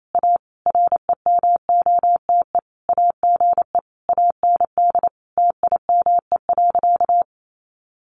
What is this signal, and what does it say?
Morse code, heard as a keyed tone, about 28 wpm, 700 Hz. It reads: A REMOTE AGE AND TIME.
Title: Sentences from Top 1300 Words 28wpm